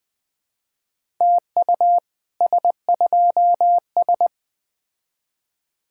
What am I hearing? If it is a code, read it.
Morse code, heard as a keyed tone, about 20 wpm, 700 Hz. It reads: TU S2S
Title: Speed-Racing Common QSO Phrases Encore - 17wpm